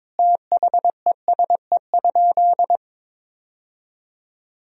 Morse code, heard as a keyed tone, about 21 wpm, 700 Hz. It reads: THESE?
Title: Sentences from Top 1000 Words - Review Part 1 - 22wpm